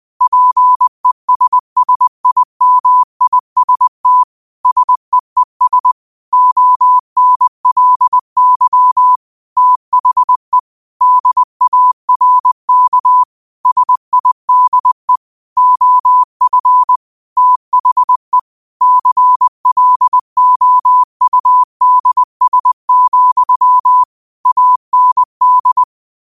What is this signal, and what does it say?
Morse code, heard as a keyed tone, about 20 wpm, 1000 Hz. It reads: PESSIMIST SEES ONLY THE DARK SIDE OF THE CLOUDS, AND
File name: qotdfiles / 20wpm